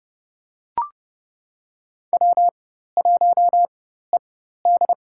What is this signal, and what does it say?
Morse code, heard as a keyed tone, about 32 wpm, spaced as for 14 wpm, 700 Hz. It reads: W1ED
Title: USA Call Signs - 4 Characters - 4x Character Spacing - 30wpm